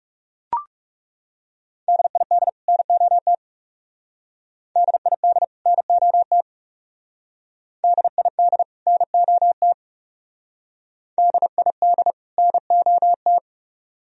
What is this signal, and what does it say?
Morse code, heard as a keyed tone, about 39 wpm, 700 Hz. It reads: DID NOT DID NOT DID NOT DID NOT